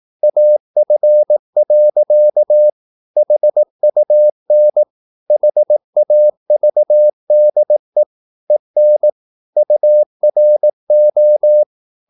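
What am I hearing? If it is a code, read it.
Morse code, heard as a keyed tone, about 18 wpm, 600 Hz. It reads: AF. HUN HAVDE EN URO
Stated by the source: Slow (QRS) Morse practice at 18 wpm, CW med Gnister.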